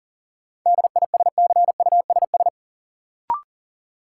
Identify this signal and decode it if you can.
Morse code, heard as a keyed tone, about 40 wpm, 700 Hz. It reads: DISCUSS
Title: Sentences from Top 1000 Words - New Word Drill - 40wpm